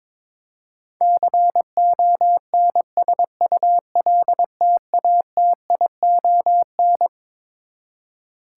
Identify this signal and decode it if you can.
Morse code, heard as a keyed tone, about 22 wpm, 700 Hz. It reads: CONSULTATION
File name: Twelve Letter Words - 22wpm